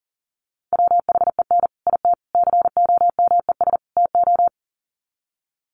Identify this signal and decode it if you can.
Morse code, heard as a keyed tone, about 40 wpm, 700 Hz. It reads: WHEN IT COMES TO
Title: Sets of 4 Words 40wpm